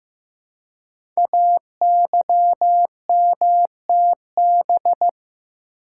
Morse code, heard as a keyed tone, about 15 wpm, 700 Hz. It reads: AYMTB
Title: Code Groups - 5 Letters - 15wpm